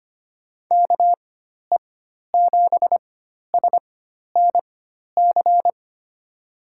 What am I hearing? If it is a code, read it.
Morse code, heard as a keyed tone, about 25 wpm, spaced as for 12 wpm, 700 Hz. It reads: KE7SNC
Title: USA Call Signs - 4x Character Spacing - 25wpm